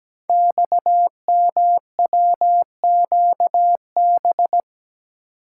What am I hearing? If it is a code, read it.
Morse code, heard as a keyed tone, about 17 wpm, 700 Hz. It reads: XMWQB
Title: Code Groups - 5 Letter:Numbers - 17wpm